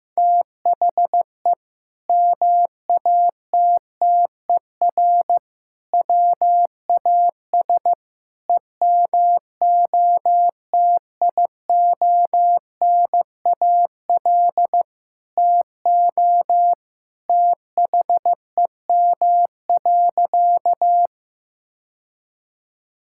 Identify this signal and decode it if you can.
Morse code, heard as a keyed tone, about 15 wpm, 700 Hz. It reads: THE MATTER WAS EMOTIONAL TO THEM.
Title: Sentences from Top 1000 Words - Review Part 5 - 15wpm